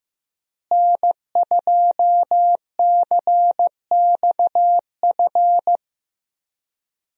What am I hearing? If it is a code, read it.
Morse code, heard as a keyed tone, about 16 wpm, 700 Hz. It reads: N2CXF
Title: USA Call Signs - 5 Characters - Repeated 3x - 15wpm